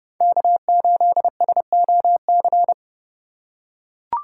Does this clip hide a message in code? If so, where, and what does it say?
Yes: Morse code from 0.2 to 2.7 s, heard as a keyed tone, about 30 wpm, 700 Hz. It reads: K8SOC